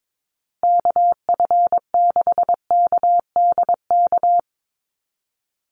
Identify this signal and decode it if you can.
Morse code, heard as a keyed tone, about 22 wpm, 700 Hz. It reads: KF6KDK
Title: USA Call Signs Repeated 3x 22wpm